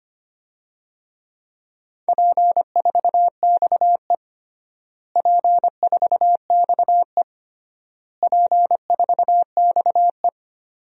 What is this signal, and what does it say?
Morse code, heard as a keyed tone, about 25 wpm, 700 Hz. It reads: P4XE P4XE P4XE
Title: DXCC-100 Call Signs - 4 Characters - Repeated 3x - 25wpm